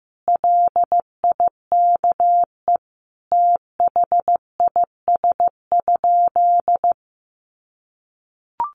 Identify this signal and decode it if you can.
Morse code, heard as a keyed tone, about 15 wpm, 700 Hz. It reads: LIKE THIS?
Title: Sentences from Top 1300 Words 15wpm